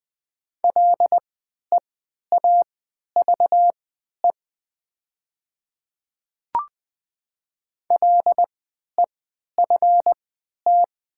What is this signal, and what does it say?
Morse code, heard as a keyed tone, about 20 wpm, spaced as for 11 wpm, 700 Hz. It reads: LEAVE LEFT
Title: 3x Character Spacing - Top 500 Words - 20wpm